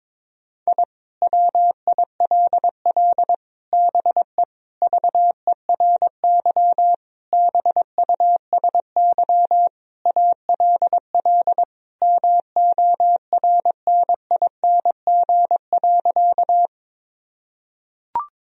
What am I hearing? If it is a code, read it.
Morse code, heard as a keyed tone, about 22 wpm, 700 Hz. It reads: I WILL BE VERY BUSY ALL MORNING.
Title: Sentences from Top 1000 Words - Review Part 3 - 22wpm